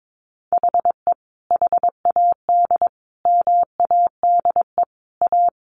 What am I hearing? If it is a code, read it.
Morse code, heard as a keyed tone, about 22 wpm, 700 Hz. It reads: HE HAD MADE A
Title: Sentences from Top 900 Words 22wpm